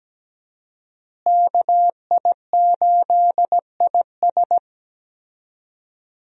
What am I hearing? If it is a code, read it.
Morse code, heard as a keyed tone, about 17 wpm, 700 Hz. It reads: KI8IS
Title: USA Call Signs - 5 Characters - Repeated 3x - 17wpm